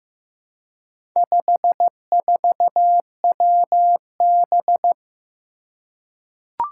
Morse code, heard as a keyed tone, about 15 wpm, 700 Hz. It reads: 54WB
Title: Four Challenging Letter-Number Combinations 15wpm